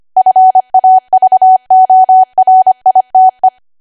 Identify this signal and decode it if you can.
Morse code, heard as a keyed tone, about 25 wpm, 750 Hz. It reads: FAVORITE
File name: Archive / 25 WPM / 25WPM